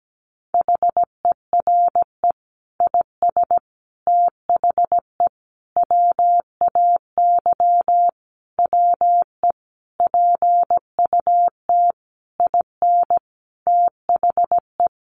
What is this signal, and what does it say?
Morse code, heard as a keyed tone, about 17 wpm, 700 Hz. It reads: HERE IS THE WAY WE PUT IN THE
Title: Sentences from Top 200 Words 17wpm